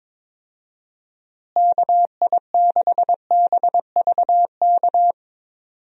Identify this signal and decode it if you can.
Morse code, heard as a keyed tone, about 22 wpm, 700 Hz. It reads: KI6BVK
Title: USA Call Signs - 6 Characters - Repeated 3x - 22wpm